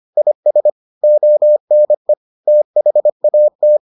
Morse code, heard as a keyed tone, about 25 wpm, 600 Hz. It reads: IS ONE THAT